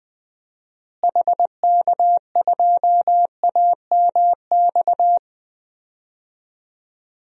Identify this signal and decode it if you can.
Morse code, heard as a keyed tone, about 20 wpm, 700 Hz. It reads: HK2AMX
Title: DXCC-100 Call Signs - 6 Characters - Repeated 2x - 20wpm